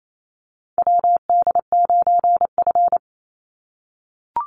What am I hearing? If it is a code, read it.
Morse code, heard as a keyed tone, about 28 wpm, 700 Hz. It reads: WD9F
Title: NA SOTA Chaser Callsigns 28wpm